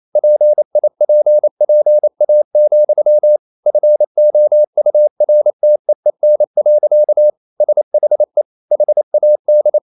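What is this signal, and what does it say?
Morse code, heard as a keyed tone, about 28 wpm, 600 Hz. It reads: PIPPA, FOURTEEN. SHE HAD
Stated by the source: Morse code practice at 28 wpm, text of Ken Follett's novel Never in English, from CW med Gnister.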